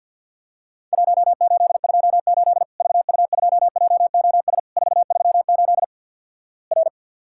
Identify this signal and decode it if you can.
Morse code, heard as a keyed tone, about 54 wpm, 700 Hz. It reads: 1828 VU2JOS 438 R
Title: Worked All Europe DX Contest - QTC Exchange - 50wpm